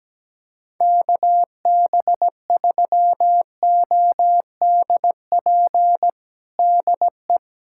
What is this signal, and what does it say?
Morse code, heard as a keyed tone, about 17 wpm, 700 Hz. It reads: KB3ODP DE
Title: USA Callsign DE Callsign 17wpm